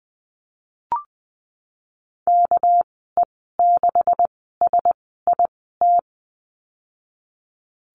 Morse code, heard as a keyed tone, about 20 wpm, spaced as for 15 wpm, 700 Hz. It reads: KE6SIT
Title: 200 Lesson 32g - Letter k - Call Signs 20wpm